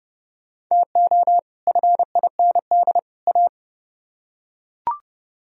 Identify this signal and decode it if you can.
Morse code, heard as a keyed tone, about 28 wpm, 700 Hz. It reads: TO FIND A